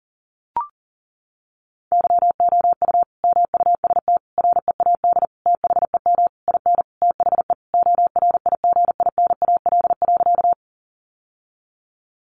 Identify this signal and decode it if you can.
Morse code, heard as a keyed tone, about 40 wpm, 700 Hz. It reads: YOU MUST READ THEM IN THE ORIGINAL.
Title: Sentences from Top 900 Words 40wpm